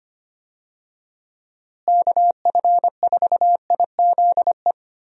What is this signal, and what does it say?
Morse code, heard as a keyed tone, about 25 wpm, 700 Hz. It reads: KF4IZE
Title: USA Call Signs - 6 Characters - Repeated 2x - 25wpm